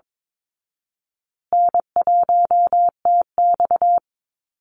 Morse code, heard as a keyed tone, about 22 wpm, 700 Hz. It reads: N1TX
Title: Easy Call Signs Repeated 3x - 22wpm